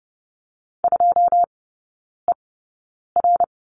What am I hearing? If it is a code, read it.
Morse code, heard as a keyed tone, about 30 wpm, spaced as for 9 wpm, 700 Hz. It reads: 2ER
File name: 081 Lesson 14e - Number 2 - Three Characters 30wpm